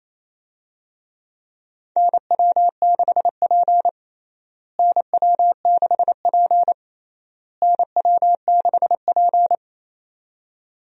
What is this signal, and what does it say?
Morse code, heard as a keyed tone, about 28 wpm, 700 Hz. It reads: NW6P NW6P NW6P
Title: USA Call Signs - 4 Characters - Repeated 3x - 28wpm